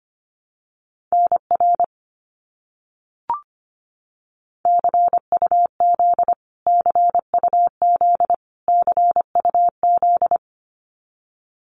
Morse code, heard as a keyed tone, about 25 wpm, 700 Hz. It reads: NR CUZ CUZ CUZ
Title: Abbreviations Repeated 3x - 25wpm